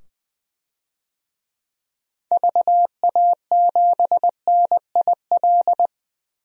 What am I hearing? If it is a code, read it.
Morse code, heard as a keyed tone, about 20 wpm, 700 Hz. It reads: VA7NIL
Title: DXCC-100 Call Signs - 6 Characters - 20wpm